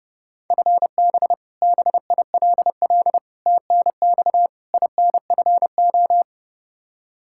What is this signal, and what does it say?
Morse code, heard as a keyed tone, about 30 wpm, 700 Hz. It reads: FB BILL TNX INFO
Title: Practice QSO - Exchange 3 - More Info - 30wpm